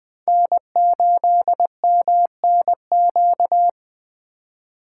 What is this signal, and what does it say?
Morse code, heard as a keyed tone, about 20 wpm, 700 Hz. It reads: N8MNQ
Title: USA Call Signs - 5 Characters - 4x Character Spacing - 20wpm